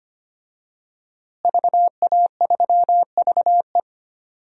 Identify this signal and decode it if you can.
Morse code, heard as a keyed tone, about 25 wpm, 700 Hz. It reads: VA3VE
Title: DXCC-100 Call Signs - 5 Characters - Repeated 3x - 25wpm